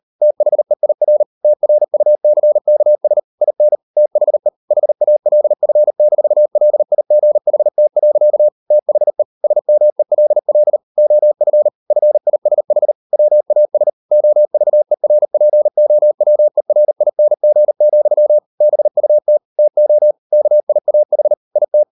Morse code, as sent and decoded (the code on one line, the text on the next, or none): - .... .. .-. / - .-. ..- -.-. -.- ... / .. -. / - .... / .... .- .-.. ..-. -....- .-.. .. --. .... - .-.-.- / - .... / ... -- . .-.. .-.. / --- ..-. / ..-. .. ... .... / .-- .- ... / --- ...- . .-. .--. --- .-- . .-. .. -. --. --..-- / -... ..- - / - --- / -.- .. .- .... / .. -
THEIR TRUCKS IN THE HALF-LIGHT. THE SMELL OF FISH WAS OVERPOWERING, BUT TO KIAH IT